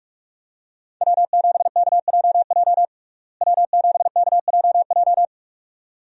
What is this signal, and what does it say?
Morse code, heard as a keyed tone, about 44 wpm, 700 Hz. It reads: W7KJJ W7KJJ
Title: USA Call Signs - 5 Characters - Repeated 2x - 45wpm